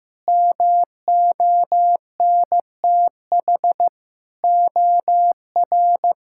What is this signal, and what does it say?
Morse code, heard as a keyed tone, about 16 wpm, 700 Hz. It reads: MONTH OR
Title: Sentences from Top 1000 Words - Review Part 5 - 15wpm